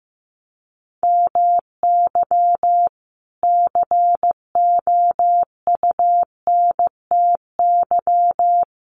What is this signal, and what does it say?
Morse code, heard as a keyed tone, about 15 wpm, 700 Hz. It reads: MY COUNTY
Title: Sentences from Top 1200 Words 15wpm